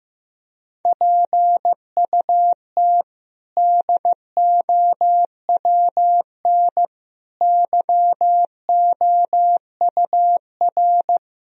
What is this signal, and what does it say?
Morse code, heard as a keyed tone, about 15 wpm, 700 Hz. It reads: PUT DOWN YOUR
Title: Sentences from Top 1000 Words - Review Part 1 - 15wpm